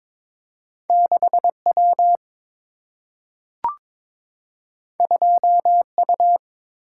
Morse code, heard as a keyed tone, about 22 wpm, 700 Hz. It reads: 6W 2U
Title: Single-Letter Single-Number 22wpm